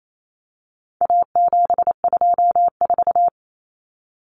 Morse code, heard as a keyed tone, about 28 wpm, 700 Hz. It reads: A724
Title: Common Cut Numbers - 4-Digit 28wpm